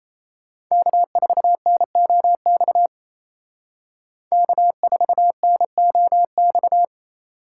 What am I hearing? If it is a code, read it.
Morse code, heard as a keyed tone, about 33 wpm, 700 Hz. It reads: K4NOX K4NOX
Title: Speed Racing - USA Call Signs 28wpm